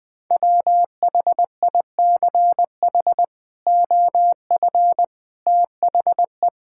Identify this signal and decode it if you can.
Morse code, heard as a keyed tone, about 20 wpm, 700 Hz. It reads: WHICH OF THE
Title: Sentences with 8-Word Phrases - 20wpm